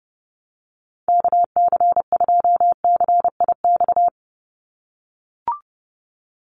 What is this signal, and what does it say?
Morse code, heard as a keyed tone, about 30 wpm, 700 Hz. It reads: KC2CIX
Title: USA Call Signs 30wpm